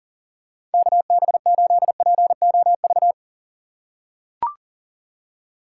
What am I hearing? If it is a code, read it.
Morse code, heard as a keyed tone, about 40 wpm, 700 Hz. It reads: KB8POV
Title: USA Call Signs - 3x Character Spacing - 40wpm